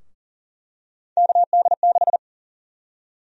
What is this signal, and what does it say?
Morse code, heard as a keyed tone, about 40 wpm, 700 Hz. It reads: KD6
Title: USA Call Sign Prefixes - 40wpm